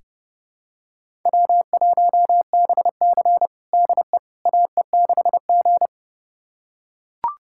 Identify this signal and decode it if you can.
Morse code, heard as a keyed tone, about 30 wpm, 700 Hz. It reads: W1BC DE AE6G
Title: USA Callsign DE Callsign - 4 Characters - 30wpm